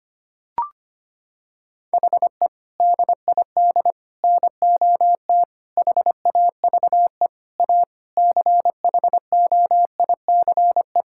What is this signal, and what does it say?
Morse code, heard as a keyed tone, about 25 wpm, 700 Hz. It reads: HE DID NOT HAVE A CHOICE
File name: practice / 25wpm / Sentences with 4-Word Phrases - 25wpm